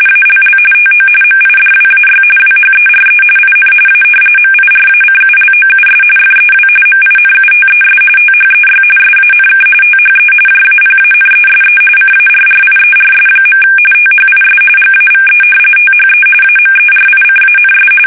RTTY, about 45.45 baud. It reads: ALL TO MIDWAY WHERE IN A MIRACULOUS FEW MINUTES THEY SANK THE JAPANESE CARRIERS. THIS GAVE THE AMERICANS
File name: fsk850